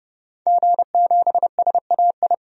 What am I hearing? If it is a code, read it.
Morse code, heard as a keyed tone, about 30 wpm, 700 Hz. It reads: G7SAI
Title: DXCC-100 Call Signs - 5 Characters - 30wpm